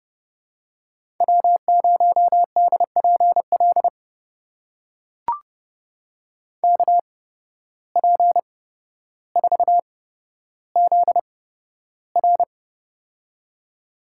Morse code, heard as a keyed tone, about 30 wpm, spaced as for 13 wpm, 700 Hz. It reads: W0DPL KP4ZR
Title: USA Call Signs - 5 Characters - 8x Character Spacing - 30wpm